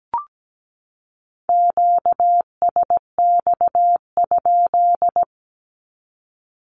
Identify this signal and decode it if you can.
Morse code, heard as a keyed tone, about 17 wpm, 700 Hz. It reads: QSX?
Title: Q-Codes 17wpm